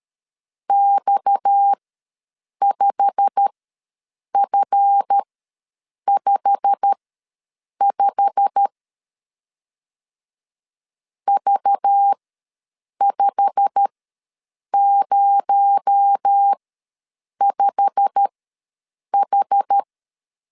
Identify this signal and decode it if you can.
Morse code, heard as a keyed tone, about 13 wpm, spaced as for 7 wpm, 800 Hz. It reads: X5F55 V505H
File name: LEZIONI / 8 YJ45 / 8 YJ45 ws 12wpm 3146